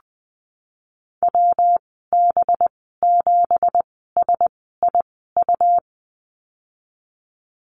Morse code, heard as a keyed tone, about 20 wpm, spaced as for 15 wpm, 700 Hz. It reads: WB7SIU
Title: 186 Lesson 30g - Letter b - Call Signs 20wpm